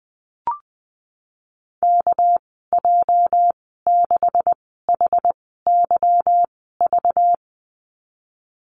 Morse code, heard as a keyed tone, about 19 wpm, spaced as for 14 wpm, 700 Hz. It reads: KJ6HYV
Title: 231 Lesson 37g - Letter x - Call Signs 20wpm